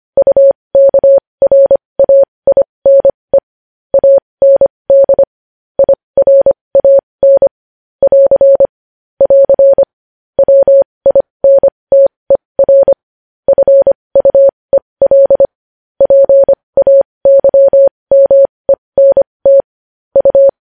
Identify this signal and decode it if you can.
Morse code, heard as a keyed tone, about 25 wpm, spaced as for 20 wpm, 550 Hz. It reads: UKRAINE AND IRAN <AR> <AR> WINTER FUEL PAYMENT U
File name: podcast / News headlines-1x-25x20